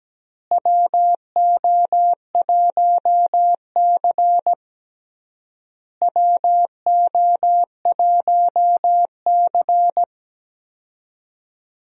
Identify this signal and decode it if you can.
Morse code, heard as a keyed tone, about 17 wpm, 700 Hz. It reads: WO1C WO1C
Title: USA Call Signs - 4 Characters - Repeated 3x - 17wpm